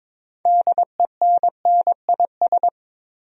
Morse code, heard as a keyed tone, about 22 wpm, 700 Hz. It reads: DENNIS
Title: USA Common Names - 22wpm